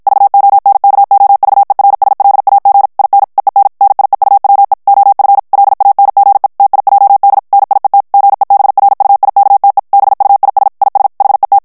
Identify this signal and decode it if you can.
Morse code, heard as a keyed tone, about 51 wpm, 800 Hz. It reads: VONKOVERSLAG IN EEN NIEUWE OF LANGE TIJD NIET GEBRUIKTE BUIS IS HET
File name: morse mp3 / HOLLANDAIS texte clair 2 vitesse 50 mots minute